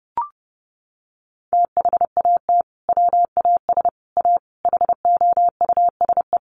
Morse code, heard as a keyed tone, about 30 wpm, 700 Hz. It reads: THAT WAS A HOUSE
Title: Sentences from Top 200 Words 30wpm